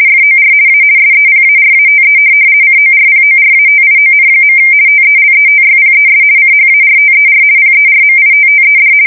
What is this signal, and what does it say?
RTTY, about 50 baud. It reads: FOR A SEPARATE CORPORATION. TELETYPES CHARTER PERMITTED
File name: fsk170-50B / teletype-corp